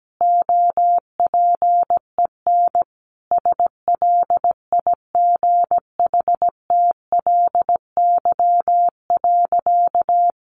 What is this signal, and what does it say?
Morse code, heard as a keyed tone, about 17 wpm, 700 Hz. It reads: OPEN SLIGHTLY.
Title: Sentences from Top 1300 Words 17wpm